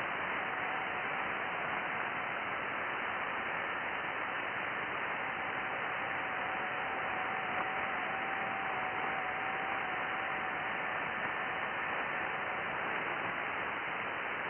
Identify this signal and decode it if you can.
FT8.
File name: IC-7100